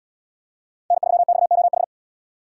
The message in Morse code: N3XOH

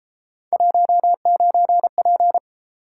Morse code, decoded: 19P

